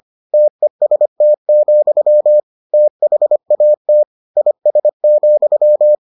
Morse code, heard as a keyed tone, about 25 wpm, 600 Hz. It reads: TEST, THAT IS,